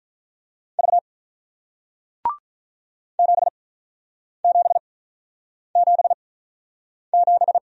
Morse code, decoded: V 7 7 7 7

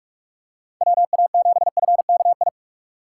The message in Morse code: WA7FKI

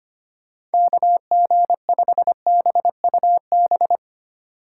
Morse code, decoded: KG5BUB

